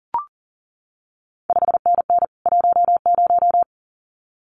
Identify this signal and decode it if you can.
Morse code, heard as a keyed tone, about 41 wpm, 700 Hz. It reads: 5NN 10
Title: CQ WW DX Contest Exchange Encore - 40wpm